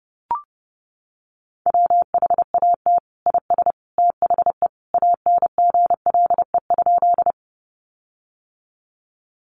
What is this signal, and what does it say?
Morse code, heard as a keyed tone, about 30 wpm, 700 Hz. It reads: WHAT IS THE ANGLE?
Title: Sentences from Top 1500 Words 30wpm